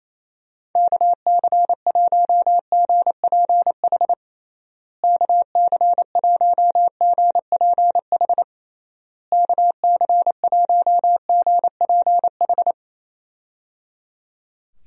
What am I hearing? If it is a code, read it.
Morse code, heard as a keyed tone, about 28 wpm, 700 Hz. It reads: KC1GPH KC1GPH KC1GPH
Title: USA Call Signs - 6 Characters - Repeated 3x - 28wpm